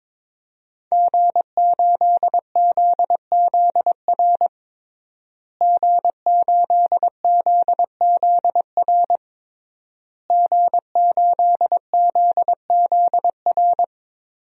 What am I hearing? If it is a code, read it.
Morse code, heard as a keyed tone, about 21 wpm, 700 Hz. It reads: G8ZZR G8ZZR G8ZZR
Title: DXCC-100 Call Signs - 5 Characters - Repeated 3x - 22wpm